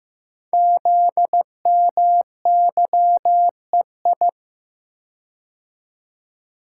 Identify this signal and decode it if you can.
Morse code, heard as a keyed tone, about 15 wpm, 690 Hz. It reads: ZMYEI
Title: Code Groups - 5 Letters - 15wpm